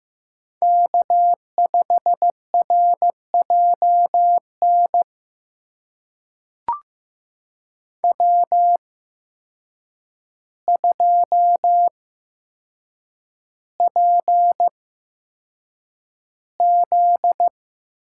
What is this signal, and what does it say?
Morse code, heard as a keyed tone, about 15 wpm, spaced as for 7 wpm, 700 Hz. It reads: K5RJN W2PZ
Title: USA Call Signs - 5 Characters - 8x Character Spacing - 15wpm